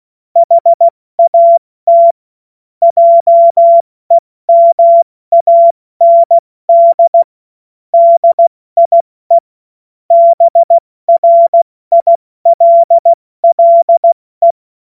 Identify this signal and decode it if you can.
Morse code, heard as a keyed tone, about 16 wpm, 680 Hz. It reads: HAT JEMAND DIE BRILLE
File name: Morsekurs / Witze / Witz 13 Februar 2024 70 F80